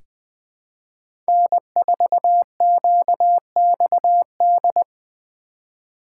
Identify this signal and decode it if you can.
Morse code, heard as a keyed tone, about 20 wpm, 700 Hz. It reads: N4QXD
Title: USA Call Signs - 5 Characters - 2x Character Spacing - 20wpm